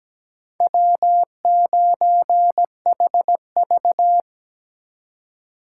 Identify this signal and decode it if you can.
Morse code, heard as a keyed tone, about 20 wpm, 700 Hz. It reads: W9HV
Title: Speed Racing - USA 1x2 Call Signs 17wpm